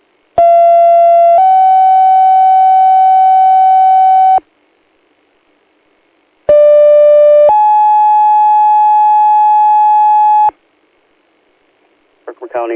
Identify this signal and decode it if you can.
Two-tone page, heard as a one-second tone, then a three-second tone, about 680 then 760 Hz, about 590 then 850 Hz. It is heard from 0.4 to 10.5 s.